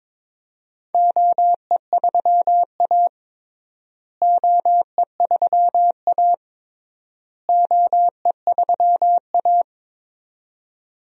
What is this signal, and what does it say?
Morse code, heard as a keyed tone, about 22 wpm, 700 Hz. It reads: OE3A OE3A OE3A
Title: DXCC-100 Call Signs - 4 Characters - Repeated 3x - 22wpm